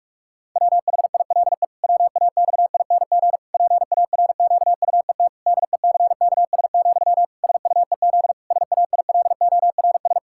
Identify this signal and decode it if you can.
Morse code, heard as a keyed tone, about 46 wpm, 700 Hz. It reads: WHILE WAXING PARQUET DECKS, SUEZ SAILORS
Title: Pangram Challenge v2 - 45wpm